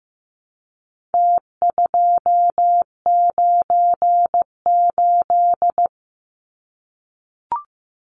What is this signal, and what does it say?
Morse code, heard as a keyed tone, about 15 wpm, 700 Hz. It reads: T298
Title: Numbers - Contest Style - 4-Digit 15wpm